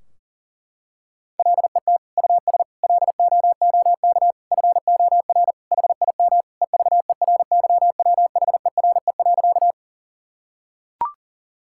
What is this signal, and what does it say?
Morse code, heard as a keyed tone, about 40 wpm, 700 Hz. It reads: LET US LOOK FOR HIM EVERYWHERE.
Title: Sentences from Top 1600 Words 40wpm